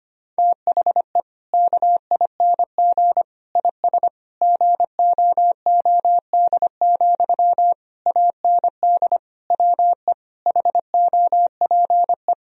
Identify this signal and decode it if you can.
Morse code, heard as a keyed tone, about 25 wpm, 700 Hz. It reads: THE KING IS GOOD, AND WE HOPE
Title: Sentences from Top 1000 Words - Review Part 1 - 25wpm